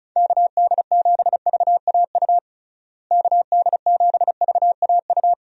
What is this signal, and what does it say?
Morse code, heard as a keyed tone, about 36 wpm, 700 Hz. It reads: KD7VAU KD7VAU
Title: USA Call Signs - 6 Characters - Repeated 2x - 35wpm